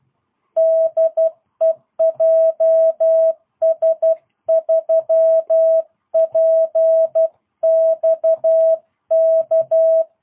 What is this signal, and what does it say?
Morse code, heard as a keyed tone, about 12 wpm, 650 Hz. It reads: DE JS3PXK
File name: jh8cbh / 2022cw